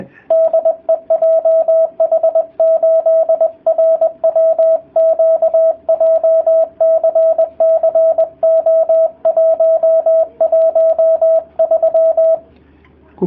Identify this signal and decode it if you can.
Morse code, heard as a keyed tone, about 20 wpm, 650 Hz. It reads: DE JH8RWQ JCC O113